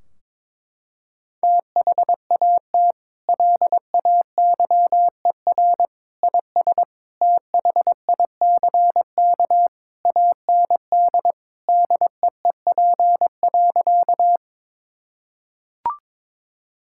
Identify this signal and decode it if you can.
Morse code, heard as a keyed tone, about 21 wpm, 700 Hz. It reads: THAT LAYER IS THICK AND DEEP.
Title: Sentences from Top 1600 Words 22wpm